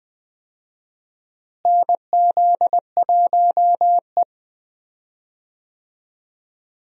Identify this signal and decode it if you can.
Morse code, heard as a keyed tone, about 20 wpm, 700 Hz. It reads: NZ1E
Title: USA Call Signs - 4 Characters - Repeated 2x - 20wpm